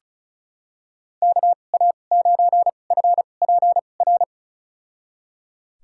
Morse code, heard as a keyed tone, about 36 wpm, spaced as for 25 wpm, 700 Hz. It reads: KA9FPR
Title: USA Call Signs - 6 Characters - 2x Character Spacing - 35wpm